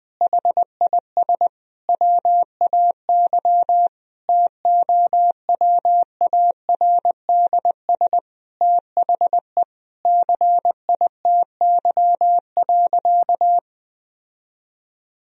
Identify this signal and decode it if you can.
Morse code, heard as a keyed tone, about 21 wpm, 700 Hz. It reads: HIS WAY TOWARDS THE CITY.
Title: Sentences from Top 1000 Words - Review Part 2 - 20wpm